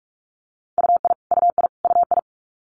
Morse code, heard as a keyed tone, about 46 wpm, 700 Hz. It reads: UI UI UI
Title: Multi-letter Phonemes Repeated 3x 45wpm